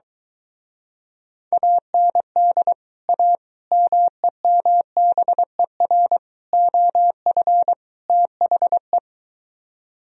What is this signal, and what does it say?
Morse code, heard as a keyed tone, about 23 wpm, 700 Hz. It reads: AND A MEMBER OF THE